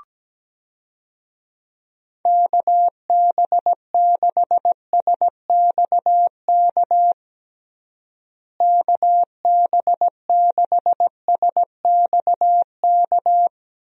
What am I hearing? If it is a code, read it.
Morse code, heard as a keyed tone, about 17 wpm, 700 Hz. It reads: KB6SXK KB6SXK
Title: USA Call Signs - 6 Characters - Repeated 2x - 17wpm